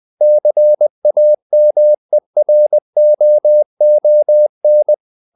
Morse code, heard as a keyed tone, about 20 wpm, 600 Hz. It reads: CAMEROON
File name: CW / Practice / NationalCapitals20